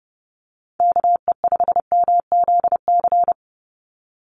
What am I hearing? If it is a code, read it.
Morse code, heard as a keyed tone, about 30 wpm, 700 Hz. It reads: KE5MZC
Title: USA Call Signs 30wpm